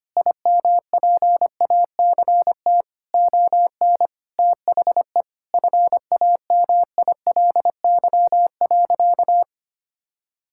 Morse code, heard as a keyed tone, about 25 wpm, 700 Hz. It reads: IMPACT ON THE FAMILY.
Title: Sentences from Top 1000 Words - Review Part 6 - 25wpm